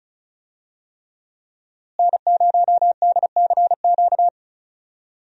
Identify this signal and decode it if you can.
Morse code, heard as a keyed tone, about 36 wpm, 700 Hz. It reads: N0DCQ
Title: USA Call Signs - 5 Characters - Repeated 3x - 35wpm